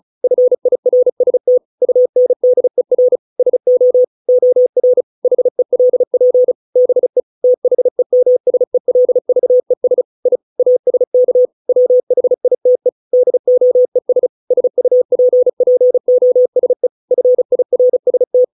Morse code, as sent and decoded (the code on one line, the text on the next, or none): ..-. .. .-. ... - / ..- -. -.. .-. / ... --- / --- .-. / .... .-.. .--. / -... / - .... -- ... .-.. ...- . ... / .. / .- ... -.- / .-- .... .. - . / -.. --- . ... / ... ..- .--. .--. --- ... / ..-. .. .-. ... -
FIRST UNDER SO OR HELP BE THEMSELVES I ASK WHITE DOES SUPPOSE FIRST